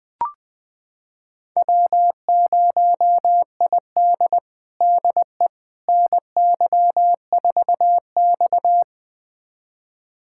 Morse code, heard as a keyed tone, about 20 wpm, 700 Hz. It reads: W0ID DE NY4X
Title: USA Callsign DE Callsign - 4 Characters - 20wpm